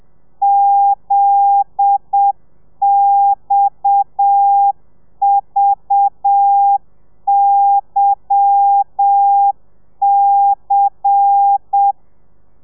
Morse code, heard as a keyed tone, about 7 wpm, 800 Hz. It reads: ZXVYC
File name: telegrafia / i2tfj / Lez10 30